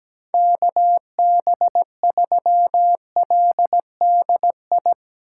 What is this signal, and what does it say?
Morse code, heard as a keyed tone, about 17 wpm, 700 Hz. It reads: KB3LDI
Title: USA Call Signs - 8x Character Spacing - 17wpm